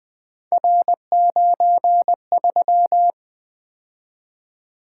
Morse code, heard as a keyed tone, about 20 wpm, 700 Hz. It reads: R93
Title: Three Letter-Number Combinations - 20wpm